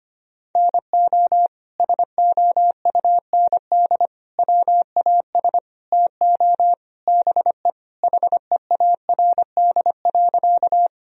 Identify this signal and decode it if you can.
Morse code, heard as a keyed tone, about 25 wpm, 700 Hz. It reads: NO SOUND WAS TO BE HEARD.
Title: Sentences from Top 1000 Words - Review Part 1 - 25wpm